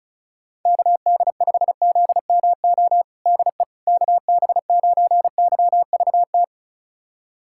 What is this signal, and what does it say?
Morse code, heard as a keyed tone, about 36 wpm, 700 Hz. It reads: KD5ZMO DE KB9YVT
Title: USA Callsign DE Callsign - 6 Characters - 35wpm